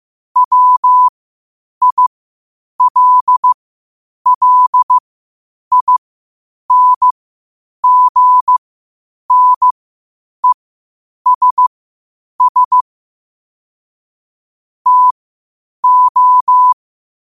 Morse code, heard as a keyed tone, about 15 wpm, spaced as for 8 wpm, 1000 Hz. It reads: WILLINGNESS TO
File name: qotdfiles / 8wpm